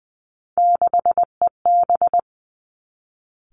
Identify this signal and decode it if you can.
Morse code, heard as a keyed tone, about 20 wpm, 700 Hz. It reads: 6EB
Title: Three Challenging Letter-Number Combinations 20wpm